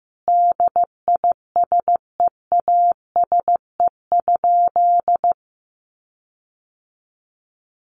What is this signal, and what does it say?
Morse code, heard as a keyed tone, about 15 wpm, 700 Hz. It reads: DISEASE?
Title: Sentences from Top 1500 Words 15wpm